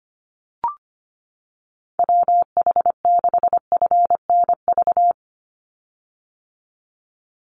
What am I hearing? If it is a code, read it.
Morse code, heard as a keyed tone, about 25 wpm, 690 Hz. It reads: WH6FNV
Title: USA Call Signs 25wpm